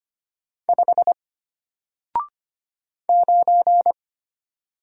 Morse code, heard as a keyed tone, about 25 wpm, 700 Hz. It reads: 5 9
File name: Single-Letter-Number-25wpm